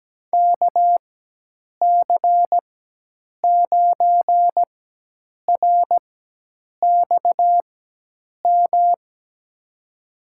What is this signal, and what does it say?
Morse code, heard as a keyed tone, about 17 wpm, spaced as for 8 wpm, 700 Hz. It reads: KC9RXM
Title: USA Call Signs - 6 Characters - 4x Character Spacing - 17wpm